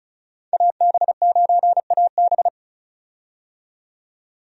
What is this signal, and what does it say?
Morse code, heard as a keyed tone, about 36 wpm, 700 Hz. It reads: AB9AB
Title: USA Call Signs - 5 Characters - 35wpm